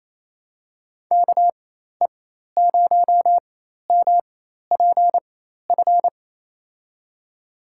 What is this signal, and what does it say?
Morse code, heard as a keyed tone, about 28 wpm, spaced as for 13 wpm, 700 Hz. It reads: KE0MPF
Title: USA Call Signs - 4x Character Spacing - 28wpm